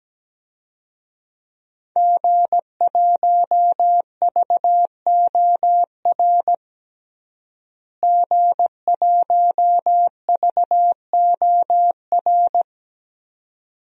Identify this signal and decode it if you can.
Morse code, heard as a keyed tone, about 17 wpm, 700 Hz. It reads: G1VOR G1VOR
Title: DXCC-100 Call Signs - 5 Characters - Repeated 2x - 17wpm